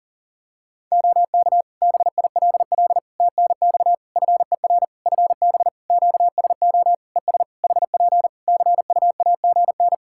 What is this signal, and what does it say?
Morse code, heard as a keyed tone, about 40 wpm, 700 Hz. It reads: OK BILL TNX FER FB QSO ES HP CUAGN
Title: Practice QSO - Exchange 4 - Ending - 40wpm